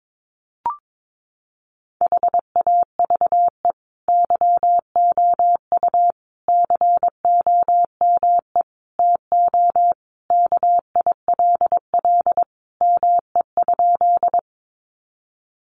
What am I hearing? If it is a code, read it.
Morse code, heard as a keyed tone, about 22 wpm, 700 Hz. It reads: HAVE YOU COME TO KILL ME?
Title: Sentences from Top 1400 Words 22wpm